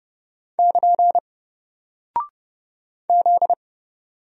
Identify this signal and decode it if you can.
Morse code, heard as a keyed tone, about 30 wpm, 700 Hz. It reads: <KN> Z
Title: Single Character (Letter:Number:Punctuation:Prosign) Alt 30wpm